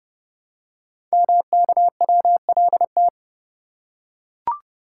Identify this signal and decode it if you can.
Morse code, heard as a keyed tone, about 30 wpm, 700 Hz. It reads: MKWLT
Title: Code Groups - 5 Letter:Numbers - 30wpm